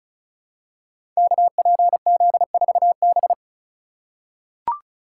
Morse code, heard as a keyed tone, about 35 wpm, 700 Hz. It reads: KPZ4B
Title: Code Groups - 5 Letter:Numbers - 35wpm